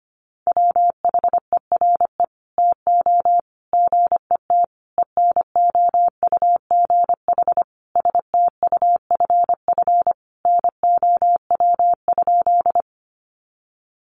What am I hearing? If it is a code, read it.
Morse code, heard as a keyed tone, about 25 wpm, 700 Hz. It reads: WHERE TO GET ENOUGH STUFF NOW?
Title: Sentences from Top 1400 Words 25wpm